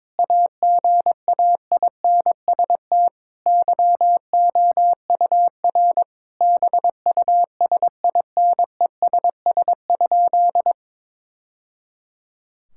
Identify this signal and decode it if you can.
Morse code, heard as a keyed tone, about 22 wpm, 700 Hz. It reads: AGAINST YOUR BUSINESS?
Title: Sentences from Top 1000 Words - Review Part 5 - 22wpm